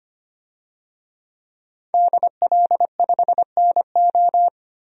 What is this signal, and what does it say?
Morse code, heard as a keyed tone, about 25 wpm, 700 Hz. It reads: DL5NO
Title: DXCC-100 Call Signs - 5 Characters - Repeated 3x - 25wpm